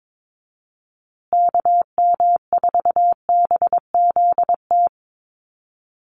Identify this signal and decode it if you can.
Morse code, heard as a keyed tone, about 22 wpm, 700 Hz. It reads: KM4BZT
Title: USA Call Signs Repeated 2x 22wpm